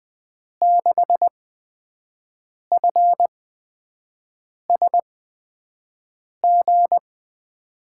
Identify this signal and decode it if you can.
Morse code, heard as a keyed tone, about 20 wpm, spaced as for 5 wpm, 700 Hz. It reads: 6FSG
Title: USA Call Signs - 8x Character Spacing - 20wpm